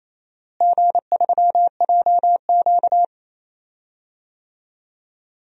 Morse code, heard as a keyed tone, about 28 wpm, 700 Hz. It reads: G3JQ
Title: DXCC-100 Call Signs - 4 Characters - Repeated 3x - 28wpm